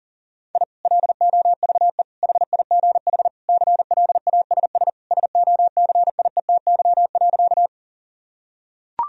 Morse code, heard as a keyed tone, about 40 wpm, 700 Hz. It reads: I LOVE HIGH CLASS SOCIETY.
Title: Sentences from Top 1000 Words - Review Part 4 - 40wpm